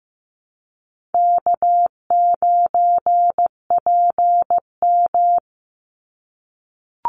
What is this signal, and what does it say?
Morse code, heard as a keyed tone, about 15 wpm, 700 Hz. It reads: K9PM
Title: NA SOTA Activator Callsigns 15wpm